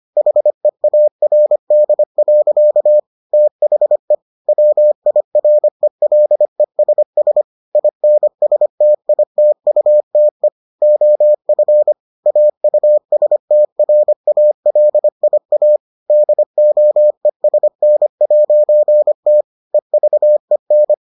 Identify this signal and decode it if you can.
Morse code, heard as a keyed tone, about 25 wpm, 600 Hz. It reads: HEARD. THE WIRELESS INSTITUTE OF AUSTRALIA DOESN'T EVEN